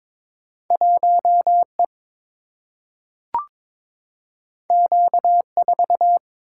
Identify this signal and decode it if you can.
Morse code, heard as a keyed tone, about 22 wpm, 700 Hz. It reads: 1E Q4
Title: Single-Letter Single-Number 22wpm